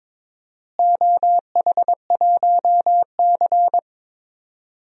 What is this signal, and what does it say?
Morse code, heard as a keyed tone, about 22 wpm, 700 Hz. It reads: OH1C
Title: DXCC-100 Call Signs - 4 Characters - Repeated 3x - 22wpm